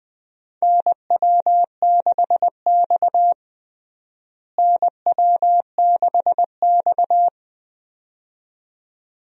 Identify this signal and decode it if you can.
Morse code, heard as a keyed tone, about 20 wpm, 700 Hz. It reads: NW6X NW6X
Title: USA Call Signs - 4 Characters - Repeated 2x - 20wpm